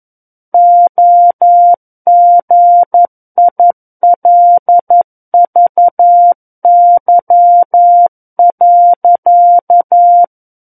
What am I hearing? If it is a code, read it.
Morse code, heard as a keyed tone, about 11 wpm, 700 Hz. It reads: OGILVY.
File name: 11 WPM